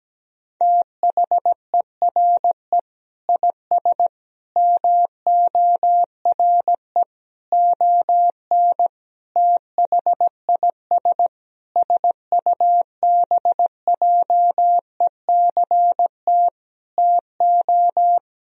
Morse code, heard as a keyed tone, about 17 wpm, 700 Hz. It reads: THERE IS MORE ON THIS SUBJECT TO
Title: Sentences from Top 1000 Words - Review Part 1 - 17wpm